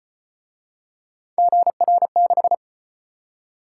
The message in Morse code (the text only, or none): GR6